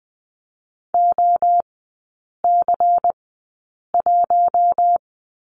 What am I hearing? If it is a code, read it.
Morse code, heard as a keyed tone, about 20 wpm, spaced as for 8 wpm, 700 Hz. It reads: OC1
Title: 100 Lesson 17e - Letter u - Three Characters 20wpm